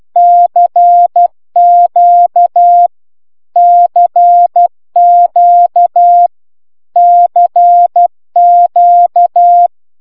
Morse code, heard as a keyed tone, about 12 wpm, 700 Hz. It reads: CQ CQ CQ
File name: cw c q